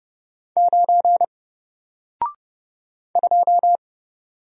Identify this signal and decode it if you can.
Morse code, heard as a keyed tone, about 30 wpm, 700 Hz. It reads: 9 2
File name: Single-Letter-Number-30wpm